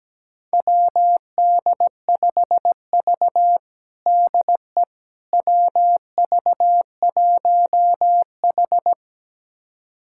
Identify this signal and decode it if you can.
Morse code, heard as a keyed tone, about 17 wpm, 700 Hz. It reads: WD5V DE WV1H
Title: USA Callsign DE Callsign - 4 Characters - 17wpm